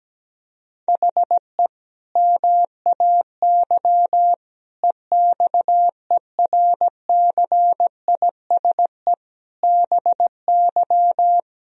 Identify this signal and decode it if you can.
Morse code, heard as a keyed tone, about 17 wpm, 700 Hz. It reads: HE MAY EXERCISE BY
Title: Sentences from Top 1000 Words - Review Part 4 - 17wpm